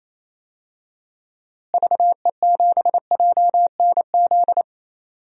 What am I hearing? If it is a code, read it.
Morse code, heard as a keyed tone, about 28 wpm, 700 Hz. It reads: VE7JNZ
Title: DXCC-100 Call Signs - 6 Characters - Repeated 2x - 28wpm